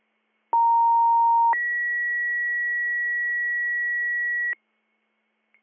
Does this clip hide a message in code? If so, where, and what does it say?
Yes: two-tone page from 0.5 to 4.5 s, heard as a one-second tone, then a three-second tone, about 930 then 1950 Hz.